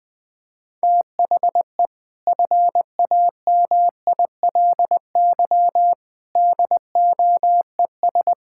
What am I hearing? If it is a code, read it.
Morse code, heard as a keyed tone, about 21 wpm, 700 Hz. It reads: THE FAMILY DOES
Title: Sentences from Top 1000 Words - Review Part 5 - 20wpm